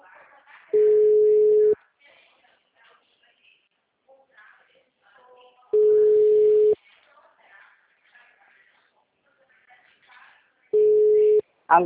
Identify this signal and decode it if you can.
Ringback tone (Europe, 425 Hz), with speech.